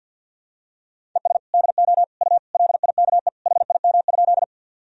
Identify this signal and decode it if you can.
Morse code, heard as a keyed tone, about 50 wpm, 700 Hz. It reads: ES DO U LIKE HIM?
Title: Sentences from Top 100 Words and Abbreviations 50wpm